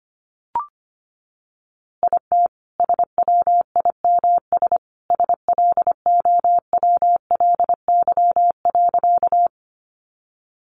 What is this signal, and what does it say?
Morse code, heard as a keyed tone, about 25 wpm, 700 Hz. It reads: IT SWIMS SLOWLY.
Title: Easy Phrases 25wpm